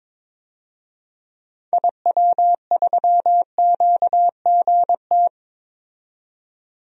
Morse code, heard as a keyed tone, about 22 wpm, 700 Hz. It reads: IW3QGT
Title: DXCC-100 Call Signs - 6 Characters - 22wpm